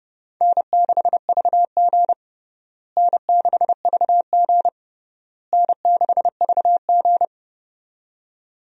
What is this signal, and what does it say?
Morse code, heard as a keyed tone, about 30 wpm, 700 Hz. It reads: N6VG N6VG N6VG
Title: USA Call Signs - 4 Characters - Repeated 3x - 30wpm